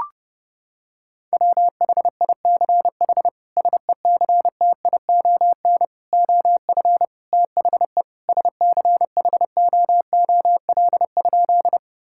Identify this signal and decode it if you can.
Morse code, heard as a keyed tone, about 30 wpm, 700 Hz. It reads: WHICH SECTION OF THE SCHOOL?
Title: Sentences from Top 1000 Words - Review Part 5 - 30wpm